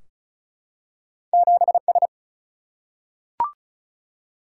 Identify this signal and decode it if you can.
Morse code, heard as a keyed tone, about 36 wpm, 700 Hz. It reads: 7S
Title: Single-Letter Single-Number 35wpm